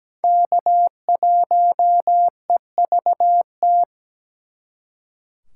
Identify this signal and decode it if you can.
Morse code, heard as a keyed tone, about 20 wpm, 700 Hz. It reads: K1EVT
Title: Speed Racing - USA 1x3 Call Signs 17wpm